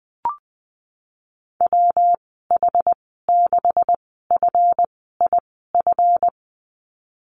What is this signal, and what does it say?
Morse code, heard as a keyed tone, about 19 wpm, spaced as for 14 wpm, 700 Hz. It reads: WH6FIF
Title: 186 Lesson 30g - Letter b - Call Signs 20wpm